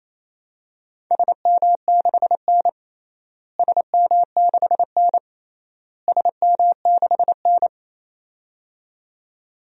Morse code, heard as a keyed tone, about 28 wpm, 700 Hz. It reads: SM6N SM6N SM6N
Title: DXCC-100 Call Signs - 4 Characters - Repeated 3x - 28wpm